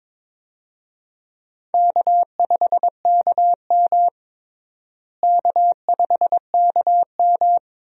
Morse code, heard as a keyed tone, about 22 wpm, 700 Hz. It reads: K5KM K5KM
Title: USA Call Signs - 4 Characters - Repeated 3x - 22wpm